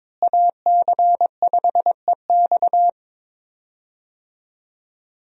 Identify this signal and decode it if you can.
Morse code, heard as a keyed tone, about 21 wpm, 700 Hz. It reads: AC5EX
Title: USA Call Signs - 5 Characters - Repeated 2x - 22wpm